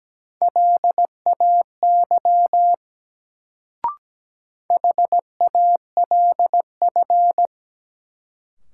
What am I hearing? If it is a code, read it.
Morse code, heard as a keyed tone, about 17 wpm, 700 Hz. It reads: LAY HALF
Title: Sentences from Top 400 Words - New Word Drill - 17wpm